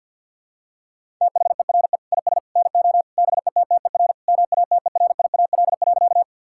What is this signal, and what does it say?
Morse code, heard as a keyed tone, about 50 wpm, 700 Hz. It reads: THERE IS NO BETTER MATERIAL.
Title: Sentences from Top 1000 Words - Review Part 5 - 50wpm